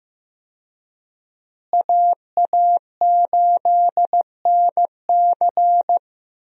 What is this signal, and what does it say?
Morse code, heard as a keyed tone, about 15 wpm, 700 Hz. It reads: AA8NC
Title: USA Call Signs - 5 Characters - Repeated 3x - 15wpm